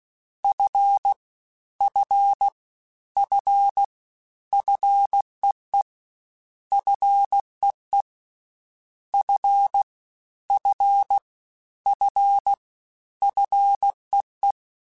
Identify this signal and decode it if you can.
Morse code, heard as a keyed tone, about 16 wpm, 800 Hz. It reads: F F F FEE FEE F F F FEE
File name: Combinations / Combinations-mp3s / Combinations@15